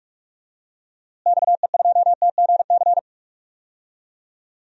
Morse code, heard as a keyed tone, about 41 wpm, 700 Hz. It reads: XE2TGC
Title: DXCC-100 Call Signs - 6 Characters - Repeated 3x - 45wpm